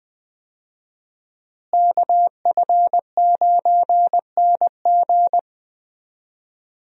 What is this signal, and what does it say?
Morse code, heard as a keyed tone, about 20 wpm, 700 Hz. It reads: KF9NG
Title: USA Call Signs - 5 Characters - Repeated 3x - 20wpm